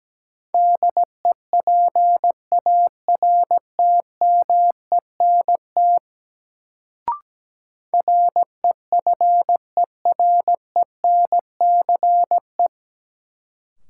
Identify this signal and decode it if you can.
Morse code, heard as a keyed tone, about 17 wpm, 700 Hz. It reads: DEPARTMENT REFERENCE
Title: Sentences from Top 1200 Words - New Word Drill Alt - 17wpm